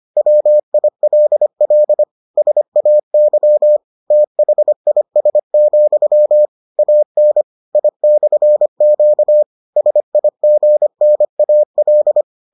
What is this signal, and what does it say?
Morse code, heard as a keyed tone, about 25 wpm, 600 Hz. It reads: WILL SAY THIS, AN I/Q SIGNAL